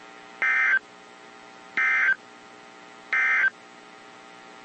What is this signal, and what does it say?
Emergency Alert System (SAME) tones, heard as harsh buzzing bursts. It is heard from 0.3 to 3.6 s.
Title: WZ2523 - NOAA Weather Radio